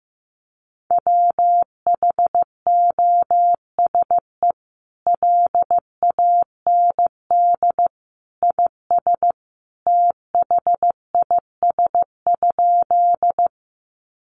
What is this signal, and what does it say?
Morse code, heard as a keyed tone, about 15 wpm, 700 Hz. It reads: WHOSE LAND IS THIS?
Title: Sentences from Top 1200 Words 15wpm